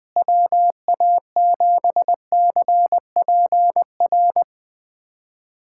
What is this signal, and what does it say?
Morse code, heard as a keyed tone, about 20 wpm, 700 Hz. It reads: WA7CPR
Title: USA Callsign DE Callsign 20wpm